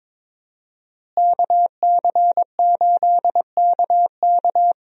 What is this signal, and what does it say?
Morse code, heard as a keyed tone, about 22 wpm, 700 Hz. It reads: KC8KK
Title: USA Call Signs - 5 Characters - Repeated 2x - 22wpm